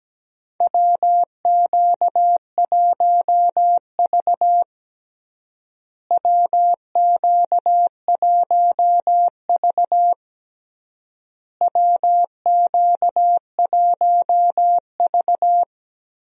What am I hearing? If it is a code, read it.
Morse code, heard as a keyed tone, about 17 wpm, 700 Hz. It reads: WQ1V WQ1V WQ1V
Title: USA Call Signs - 4 Characters - Repeated 3x - 17wpm